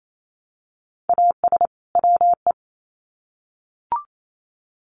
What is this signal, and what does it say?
Morse code, heard as a keyed tone, about 28 wpm, 700 Hz. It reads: AS WE
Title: Sets of 2 Words 28wpm